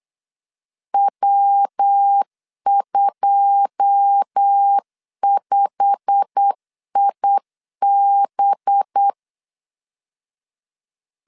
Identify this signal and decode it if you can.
Morse code, heard as a keyed tone, about 8 wpm, 800 Hz. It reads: W25IB
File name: CW / DET / 1 9 1500C S C 8wpm 5653